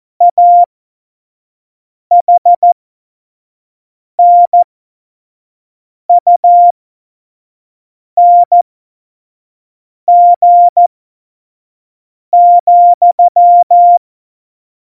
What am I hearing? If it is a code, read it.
Morse code, heard as a keyed tone, about 14 wpm, spaced as for 5 wpm, 700 Hz. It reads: AHNUNG,
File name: Morsekurs / Witze / Witz 02. April 2019, 25 F70